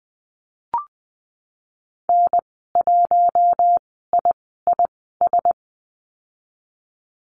Morse code, heard as a keyed tone, about 20 wpm, spaced as for 15 wpm, 700 Hz. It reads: N1IIS
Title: 048 Lesson 9g - Number 4 - Call Signs 20wpm